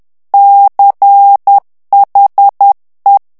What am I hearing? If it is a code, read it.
Morse code, heard as a keyed tone, about 10 wpm, 800 Hz. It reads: CHE
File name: ITALIEN texte clair 2 vitesse 10 mots minute